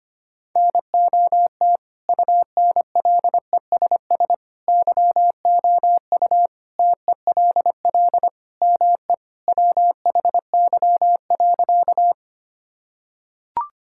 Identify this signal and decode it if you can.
Morse code, heard as a keyed tone, about 25 wpm, 700 Hz. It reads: NOT UNLESS YOU TELL ME WHY.
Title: Sentences from Top 1000 Words - Review Part 6 - 25wpm